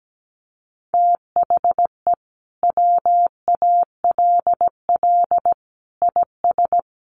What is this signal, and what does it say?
Morse code, heard as a keyed tone, about 17 wpm, 700 Hz. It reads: THE WALL IS
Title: Sentences from Top 1400 Words 17wpm